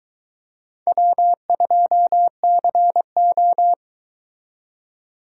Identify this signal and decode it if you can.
Morse code, heard as a keyed tone, about 23 wpm, 700 Hz. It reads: W2CO